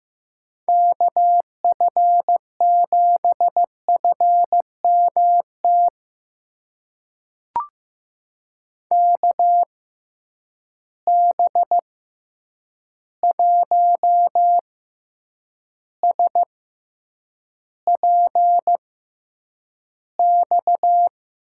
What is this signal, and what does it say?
Morse code, heard as a keyed tone, about 15 wpm, spaced as for 8 wpm, 700 Hz. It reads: KF7FMT KB1SPX
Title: USA Call Signs - 6 Characters - 6x Character Spacing - 15wpm